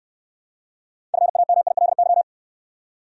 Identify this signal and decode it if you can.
Morse code, heard as a keyed tone, about 68 wpm, 700 Hz. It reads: FAMILY